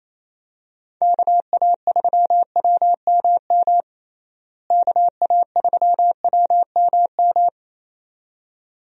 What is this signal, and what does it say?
Morse code, heard as a keyed tone, about 28 wpm, 700 Hz. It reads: KA3WMM KA3WMM
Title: USA Call Signs - 6 Characters - Repeated 2x - 28wpm